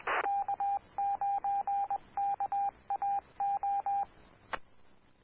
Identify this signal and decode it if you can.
Morse code, heard as a keyed tone, about 21 wpm, 800 Hz, with radio noise. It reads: K9KAO
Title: Claremont, CA Police, Amateur Radio